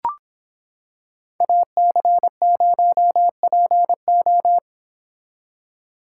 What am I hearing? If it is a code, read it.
Morse code, heard as a keyed tone, about 26 wpm, 700 Hz. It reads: AC0PO